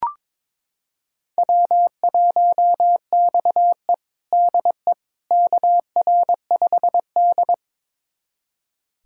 Morse code, heard as a keyed tone, about 22 wpm, 700 Hz. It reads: W1XE DE KR5D
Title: USA Callsign DE Callsign - 4 Characters - 22wpm